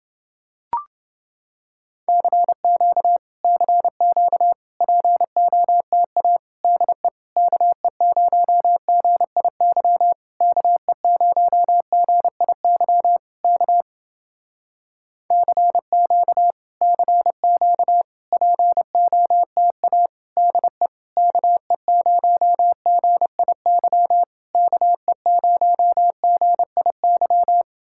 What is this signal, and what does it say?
Morse code, heard as a keyed tone, about 30 wpm, 700 Hz. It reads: CQ CQ POTA DE KE0GIY KE0GIY K CQ CQ POTA DE KE0GIY KE0GIY